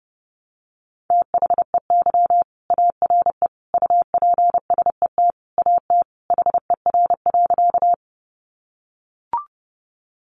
Morse code, heard as a keyed tone, about 30 wpm, 700 Hz. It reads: THEY ARE UPSET AT HER.
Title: Sentences from Top 800 Words 30wpm